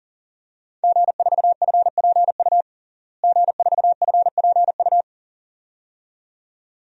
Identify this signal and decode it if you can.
Morse code, heard as a keyed tone, about 40 wpm, 700 Hz. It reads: G4FPU G4FPU
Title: DXCC-100 Call Signs - 5 Characters - Repeated 2x - 40wpm